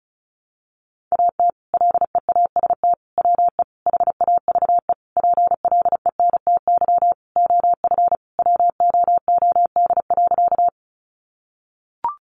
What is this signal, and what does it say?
Morse code, heard as a keyed tone, about 34 wpm, 700 Hz. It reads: AT LEAST WE HAVE PLENTY OF WOOD.
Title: Sentences from Top 1300 Words 35wpm